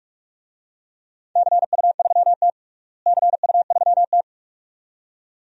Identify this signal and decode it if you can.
Morse code, heard as a keyed tone, about 46 wpm, 700 Hz. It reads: CU3T CU3T
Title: DXCC-100 Call Signs - 4 Characters - Repeated 2x - 45wpm